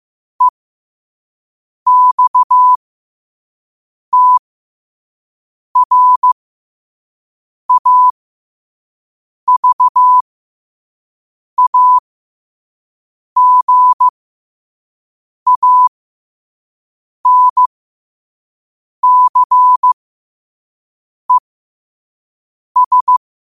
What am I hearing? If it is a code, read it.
Morse code, heard as a keyed tone, about 15 wpm, spaced as for 5 wpm, 1000 Hz. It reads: EXTRAVAGANCES
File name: qotdfiles / 5wpm